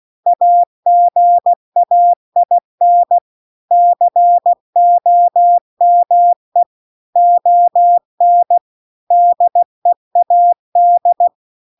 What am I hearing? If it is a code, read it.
Morse code, heard as a keyed tone, about 16 wpm, 700 Hz. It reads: AGAIN COME ON DEAD